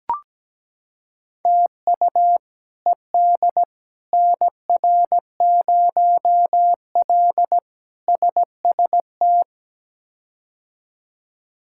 Morse code, heard as a keyed tone, about 17 wpm, 700 Hz. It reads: TU ED NR0L SST
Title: K1USN SST QSO Practice - 17wpm